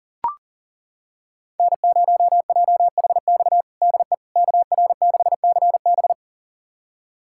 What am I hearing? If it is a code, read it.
Morse code, heard as a keyed tone, about 40 wpm, 700 Hz. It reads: N0JHX DE KR6CB
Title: USA Callsign DE Callsign - 5 Characters - 40wpm